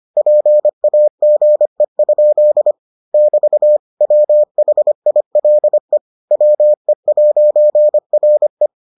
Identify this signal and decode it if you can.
Morse code, heard as a keyed tone, about 25 wpm, 600 Hz. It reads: PAGE? <BT> WHILE WE'RE